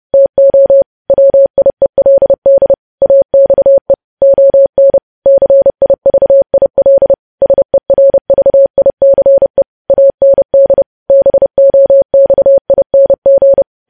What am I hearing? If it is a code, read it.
Morse code, heard as a keyed tone, about 30 wpm, 550 Hz. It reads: TO WIELD AXE ON CIVIL SERVICE AND BOXING
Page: news headlines in morse code at 30 wpm